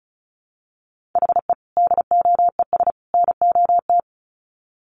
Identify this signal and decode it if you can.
Morse code, heard as a keyed tone, about 34 wpm, 700 Hz. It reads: HE DOES NOT
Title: Sets of 3 Words 35wpm